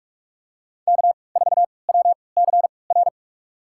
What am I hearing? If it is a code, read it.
Morse code, heard as a keyed tone, about 46 wpm, spaced as for 26 wpm, 700 Hz. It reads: K4WCR
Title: USA Call Signs - 3x Character Spacing - 45wpm